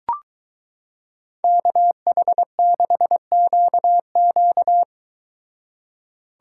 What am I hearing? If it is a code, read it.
Morse code, heard as a keyed tone, about 23 wpm, 700 Hz. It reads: KH6QQ